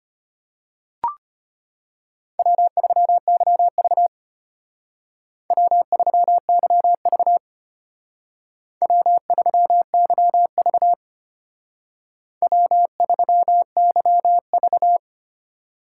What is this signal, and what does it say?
Morse code, heard as a keyed tone, about 33 wpm, 700 Hz. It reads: W3YV W3YV W3YV W3YV